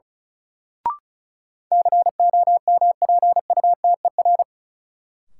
Morse code, heard as a keyed tone, about 36 wpm, 700 Hz. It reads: COMPUTER
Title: Sentences from Top 1000 Words - Review Part 3 - New Word Drill - 35wpm